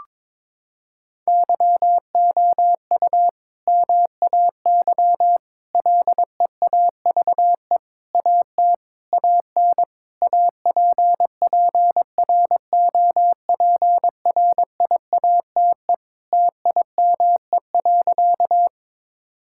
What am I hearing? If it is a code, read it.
Morse code, heard as a keyed tone, about 21 wpm, 700 Hz. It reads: YOU MAY LEAVE AT AN APPROPRIATE TIME.
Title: Sentences from Top 1000 Words - Review Part 5 - 22wpm